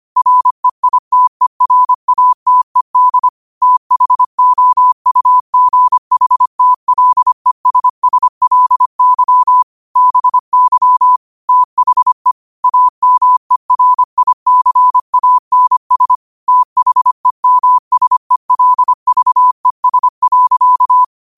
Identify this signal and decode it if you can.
Morse code, heard as a keyed tone, about 25 wpm, 1000 Hz. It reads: REITERATED THOUGHTLESSLY BY THE AMERICANS THEMSELVES.
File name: qotdfiles / 25wpm